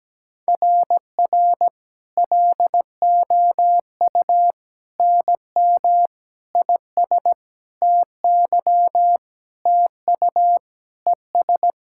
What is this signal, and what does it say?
Morse code, heard as a keyed tone, about 17 wpm, 700 Hz. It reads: RR LOU NM IS TY TU ES
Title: SOTA QSO Practice 17wpm